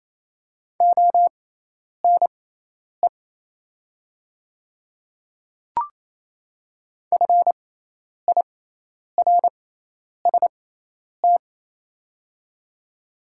Morse code, heard as a keyed tone, about 28 wpm, spaced as for 10 wpm, 700 Hz. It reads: ONE FIRST
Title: 6x Character Spacing - Top 100 Words - 28wpm